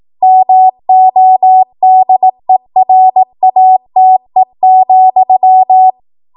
Morse code, heard as a keyed tone, about 18 wpm, 750 Hz. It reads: MODERATE,
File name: Morse / 18WPM